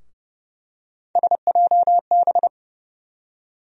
Morse code, heard as a keyed tone, about 30 wpm, 700 Hz. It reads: SJB
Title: USA Call Sign Postfixes - 3 Letters 30wpm